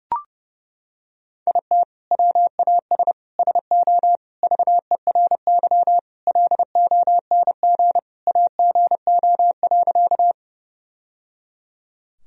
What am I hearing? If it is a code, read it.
Morse code, heard as a keyed tone, about 30 wpm, 700 Hz. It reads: IT WAS SO VERY LONG AGO.
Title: Sentences from Top 1000 Words - Review Part 3 - 30wpm